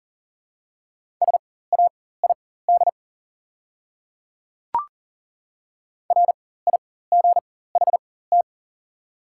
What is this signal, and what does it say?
Morse code, heard as a keyed tone, about 40 wpm, spaced as for 19 wpm, 700 Hz. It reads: SAID RIGHT
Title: 4x Character Spacing - Top 200 Words - 40wpm